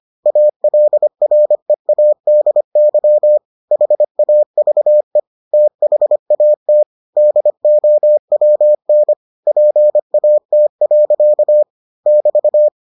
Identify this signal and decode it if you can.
Morse code, heard as a keyed tone, about 25 wpm, 600 Hz. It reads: ALREADY HAVE THAT DOWN PAT. <BT>